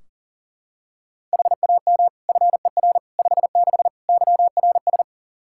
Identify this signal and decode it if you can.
Morse code, heard as a keyed tone, about 40 wpm, 700 Hz. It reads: HAM FER 56 YRS
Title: Practice QSO - Exchange 3 - More Info - 40wpm